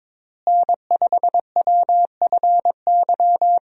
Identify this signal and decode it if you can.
Morse code, heard as a keyed tone, about 22 wpm, 700 Hz. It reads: N5WFY
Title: USA Call Signs - 5 Characters - Repeated 3x - 22wpm